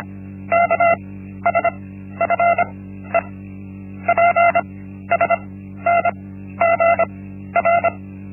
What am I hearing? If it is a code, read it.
Morse code, heard as a keyed tone, about 25 wpm, 680 Hz. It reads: K S F E P S N G R
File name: Linksmakalnis-1